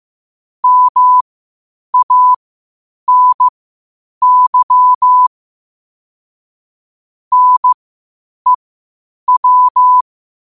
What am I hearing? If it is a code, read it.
Morse code, heard as a keyed tone, about 15 wpm, spaced as for 8 wpm, 1000 Hz. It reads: MANY NEW
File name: qotdfiles / 8wpm